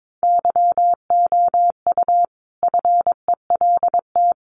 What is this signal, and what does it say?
Morse code, heard as a keyed tone, about 22 wpm, 700 Hz. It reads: YOU FELT
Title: A Sound Of Thunder 22wpm